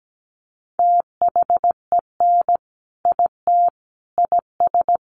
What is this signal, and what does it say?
Morse code, heard as a keyed tone, about 17 wpm, 700 Hz. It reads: THEN IT IS
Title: Sentences from Top 500 Words 17wpm